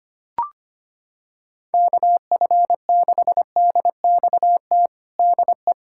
Morse code, heard as a keyed tone, about 25 wpm, 700 Hz. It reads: KF6DXT DE
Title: USA Callsign DE Callsign - 6 Characters - 25wpm